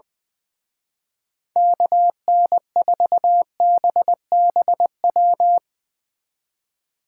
Morse code, heard as a keyed tone, about 22 wpm, 700 Hz. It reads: KN4BBW
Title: Speed Racing - USA 2x3 Call Signs 20wpm